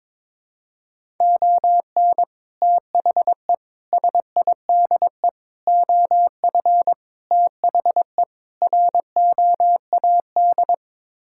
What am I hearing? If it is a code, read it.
Morse code, heard as a keyed tone, about 22 wpm, 700 Hz. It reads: ON THE SIDE OF THE ROAD
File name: practice / 22wpm / Sets of 6 Words - 22wpm